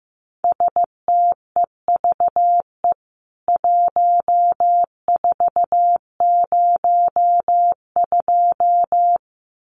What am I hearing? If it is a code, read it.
Morse code, heard as a keyed tone, about 15 wpm, 700 Hz. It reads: STEVE 1402
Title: CWT Exchange 15wpm